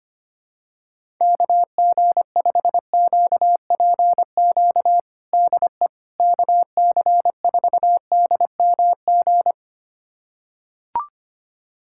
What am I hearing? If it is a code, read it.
Morse code, heard as a keyed tone, about 25 wpm, 700 Hz. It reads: KG5QPQ DE KC4DMG
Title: USA Callsign DE Callsign - 6 Characters - 25wpm